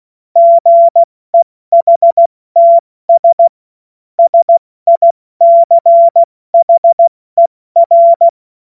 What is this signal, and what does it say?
Morse code, heard as a keyed tone, about 16 wpm, 680 Hz. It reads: GEHTS SICHER
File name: Morsekurs / Witze / Witz 20. Januar 2026 70 F80